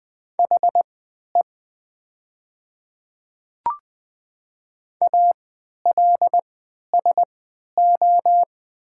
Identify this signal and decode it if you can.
Morse code, heard as a keyed tone, about 20 wpm, spaced as for 11 wpm, 700 Hz. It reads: HE ALSO
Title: 3x Character Spacing - Top 200 Words - 20wpm